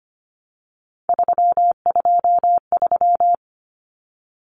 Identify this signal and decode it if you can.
Morse code, heard as a keyed tone, about 25 wpm, 700 Hz. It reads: 323
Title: Common Cut Numbers - 3-Digit 25wpm